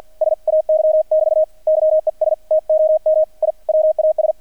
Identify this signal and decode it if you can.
Morse code, heard as a keyed tone, about 48 wpm, 620 Hz. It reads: SAY <BT> YES TOM I WAS
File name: archived / CRSnet